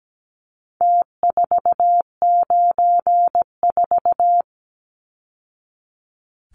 Morse code, heard as a keyed tone, about 17 wpm, 700 Hz. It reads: T494
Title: Numbers - Contest Style - 4-Digit 17wpm